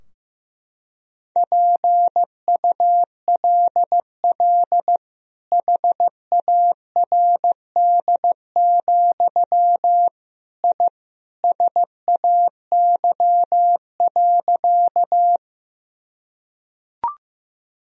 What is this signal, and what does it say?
Morse code, heard as a keyed tone, about 16 wpm, 700 Hz. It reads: PULL HARD, I SAY.
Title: Sentences from Top 1000 Words - Review Part 2 - 15wpm